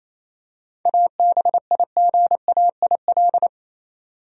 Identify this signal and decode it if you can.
Morse code, heard as a keyed tone, about 28 wpm, 700 Hz. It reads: ABIGAIL
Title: USA Common Names - 28wpm